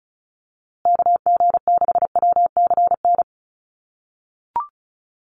Morse code, heard as a keyed tone, about 34 wpm, 700 Hz. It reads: KG6WCN
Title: USA Call Signs Repeated 3x 35wpm